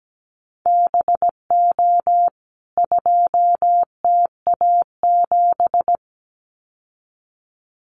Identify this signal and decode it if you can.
Morse code, heard as a keyed tone, about 17 wpm, 700 Hz. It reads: BO 2TA7
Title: CWT Exchange 17wpm